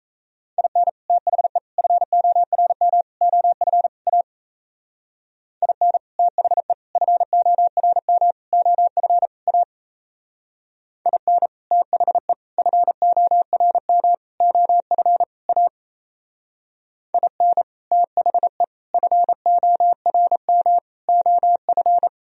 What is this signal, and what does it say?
Morse code, heard as a keyed tone, about 34 wpm, 700 Hz. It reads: IN THE FORM OF A IN THE FORM OF A IN THE FORM OF A IN THE FORM OF
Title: Speed-Racing - Sets of 5 Words 28wpm